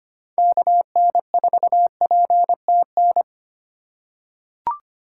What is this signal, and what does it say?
Morse code, heard as a keyed tone, about 25 wpm, 700 Hz. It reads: KN4PTN
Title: USA Call Signs - 6 Characters - 2x Character Spacing - 25wpm